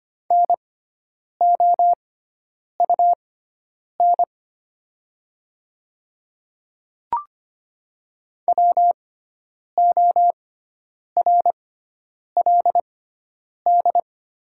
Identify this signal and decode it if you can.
Morse code, heard as a keyed tone, about 25 wpm, spaced as for 9 wpm, 700 Hz. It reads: NOUN WORLD
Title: 6x Character Spacing - Top 500 Words - 25wpm